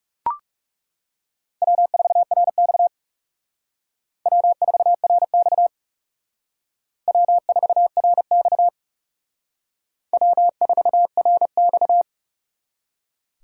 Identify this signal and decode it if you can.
Morse code, heard as a keyed tone, about 40 wpm, 700 Hz. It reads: W4RX W4RX W4RX W4RX